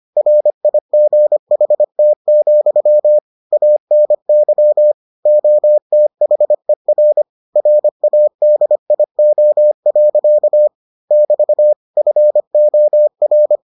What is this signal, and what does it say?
Morse code, heard as a keyed tone, about 25 wpm, 600 Hz. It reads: RIGHT, ANY OTHER RADIO. <BT> FOR